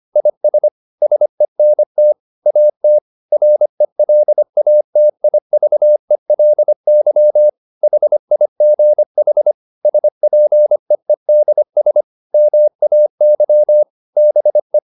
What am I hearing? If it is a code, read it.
Morse code, heard as a keyed tone, about 25 wpm, 600 Hz. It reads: IS SENT AT RELATIVELY HIGH SPEEDS MAY BE